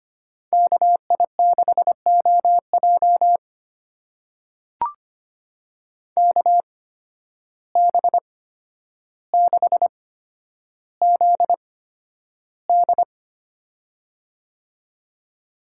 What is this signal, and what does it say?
Morse code, heard as a keyed tone, about 25 wpm, spaced as for 11 wpm, 700 Hz. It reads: KI6OJ KB6ZD
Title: USA Call Signs - 5 Characters - 8x Character Spacing - 25wpm